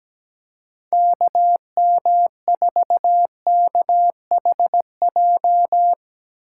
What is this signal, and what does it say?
Morse code, heard as a keyed tone, about 17 wpm, 700 Hz. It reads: KM4KHJ
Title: USA Call Signs - 6 Characters - 3x Character Spacing - 17wpm